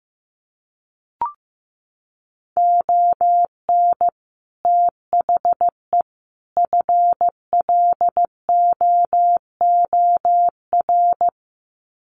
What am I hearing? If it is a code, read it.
Morse code, heard as a keyed tone, about 15 wpm, 700 Hz. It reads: ON THE FLOOR
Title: Sets of 3 Words 15wpm